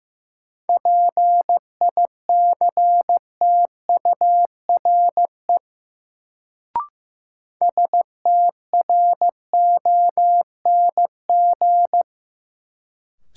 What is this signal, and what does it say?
Morse code, heard as a keyed tone, about 15 wpm, 700 Hz. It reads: PICTURE STRONG
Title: Sentences from Top 500 Words - New Word Drill - 15wpm